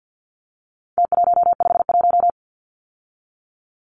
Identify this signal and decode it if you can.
Morse code, heard as a keyed tone, about 50 wpm, 700 Hz. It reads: T151
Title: Numbers - Contest Style - 4-Digit 50wpm